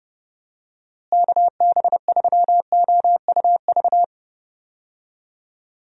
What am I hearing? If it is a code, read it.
Morse code, heard as a keyed tone, about 32 wpm, 700 Hz. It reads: KB3OUV
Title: USA Call Signs - 6 Characters - Repeated 2x - 30wpm